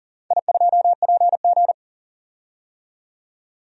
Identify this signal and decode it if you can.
Morse code, heard as a keyed tone, about 40 wpm, 700 Hz. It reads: I2PG
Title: DXCC-100 Call Signs - 4 Characters - 40wpm